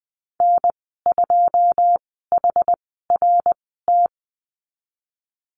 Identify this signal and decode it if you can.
Morse code, heard as a keyed tone, about 20 wpm, spaced as for 15 wpm, 700 Hz. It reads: N2HRT
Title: 088 Lesson 15g - Number 5 - Call Signs 20wpm